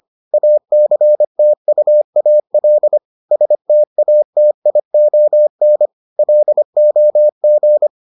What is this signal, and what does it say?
Morse code, heard as a keyed tone, about 25 wpm, 600 Hz. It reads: ACTUAL STATION LOG